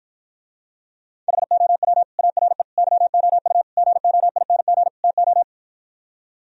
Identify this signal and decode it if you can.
Morse code, heard as a keyed tone, about 60 wpm, 700 Hz. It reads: HOW ARE YOU GOING TO